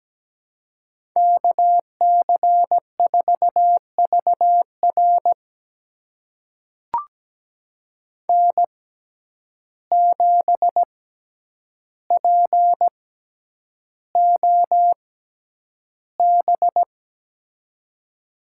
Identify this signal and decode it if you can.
Morse code, heard as a keyed tone, about 17 wpm, spaced as for 9 wpm, 700 Hz. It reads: KC4VR N7POB
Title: USA Call Signs - 6x Character Spacing - 17wpm